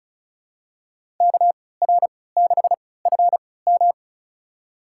Morse code, heard as a keyed tone, about 36 wpm, spaced as for 20 wpm, 700 Hz. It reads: KR6FM
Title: USA Call Signs - 5 Characters - 3x Character Spacing - 35wpm